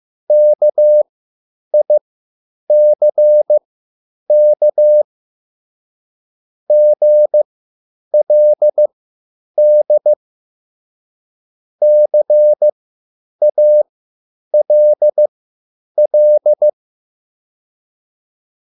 Morse code, heard as a keyed tone, about 15 wpm, spaced as for 9 wpm, 600 Hz. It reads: KICK GLD CALL